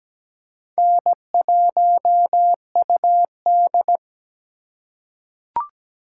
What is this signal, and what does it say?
Morse code, heard as a keyed tone, about 17 wpm, 700 Hz. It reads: N1UD
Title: USA Call Signs - 4 Characters - 3x Character Spacing - 17wpm